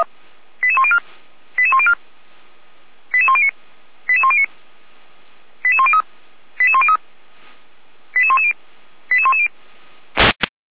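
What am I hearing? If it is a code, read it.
Five-tone selective call (ZVEI), heard as quick runs of five tones, tones of about 70 ms. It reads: 88104 88104 88109 88109 88103 88103 88110 88110
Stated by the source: ZVEI five-tone call received by SDR, from the Feuerwehr Orlamünde audio server.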